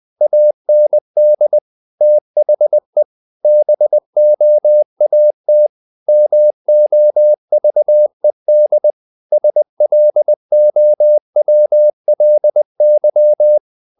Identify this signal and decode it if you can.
Morse code, heard as a keyed tone, about 20 wpm, 600 Hz. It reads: AND THE BOAT MOVED SLOWLY
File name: the old man and the sea / the old man and the sea part 1 20wpm